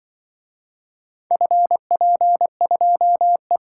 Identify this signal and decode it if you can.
Morse code, heard as a keyed tone, about 24 wpm, 700 Hz. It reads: FP2E
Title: Speed-Racing - Four Letter-Number Combinations - 20wpm